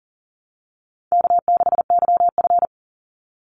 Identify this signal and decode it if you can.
Morse code, heard as a keyed tone, about 40 wpm, 700 Hz. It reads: K6YF
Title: Easy Call Signs Repeated 3x - 40wpm